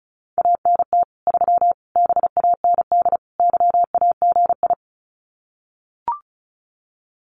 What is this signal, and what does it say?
Morse code, heard as a keyed tone, about 34 wpm, 700 Hz. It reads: ANT 3 BAND YAGI
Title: Common QSO Phrases 35wpm